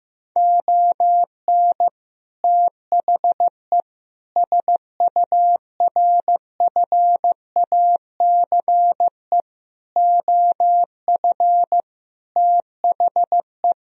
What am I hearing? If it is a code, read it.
Morse code, heard as a keyed tone, about 15 wpm, 700 Hz. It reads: ON THE SURFACE OF THE
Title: Sentences from Top 1000 Words - Review Part 3 - 15wpm